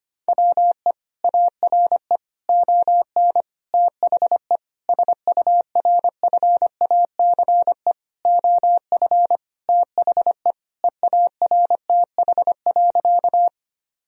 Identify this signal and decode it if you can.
Morse code, heard as a keyed tone, about 25 wpm, 700 Hz. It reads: WE ARE ON THE SURFACE OF THE EARTH.
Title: Sentences from Top 1000 Words - Review Part 3 - 25wpm